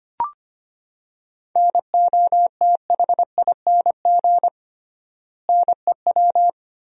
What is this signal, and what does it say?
Morse code, heard as a keyed tone, about 25 wpm, 700 Hz. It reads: NOTHING NEW
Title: 3x Word-Spacing - Sentences from Top 500 Words - 25wpm